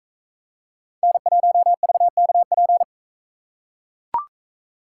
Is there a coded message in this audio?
Yes: Morse code from 1.0 to 2.8 s, heard as a keyed tone, about 41 wpm, 700 Hz. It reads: N1VKP